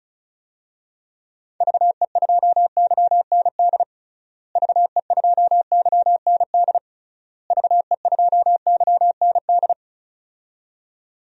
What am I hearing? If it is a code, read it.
Morse code, heard as a keyed tone, about 36 wpm, 700 Hz. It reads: VE2YND VE2YND VE2YND
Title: DXCC-100 Call Signs - 6 Characters - Repeated 3x - 35wpm